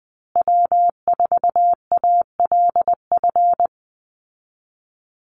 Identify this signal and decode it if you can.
Morse code, heard as a keyed tone, about 19 wpm, 700 Hz. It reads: W4ALF
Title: Call Signs Repeated 2x - 20wpm